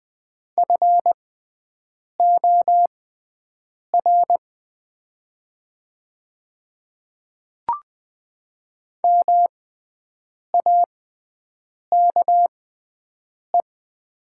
Morse code, heard as a keyed tone, about 20 wpm, spaced as for 7 wpm, 700 Hz. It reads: FOR MAKE
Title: 6x Character Spacing - Top 300 Words - 20wpm